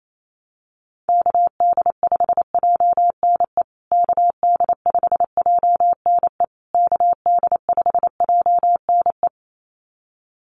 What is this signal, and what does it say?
Morse code, heard as a keyed tone, about 28 wpm, 700 Hz. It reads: KD5JNE KD5JNE KD5JNE
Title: USA Call Signs Repeated 3x 28wpm